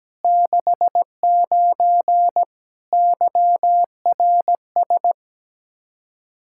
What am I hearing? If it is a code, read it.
Morse code, heard as a keyed tone, about 17 wpm, 700 Hz. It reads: 69 YRS
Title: K1USN SST QSO Practice 17wpm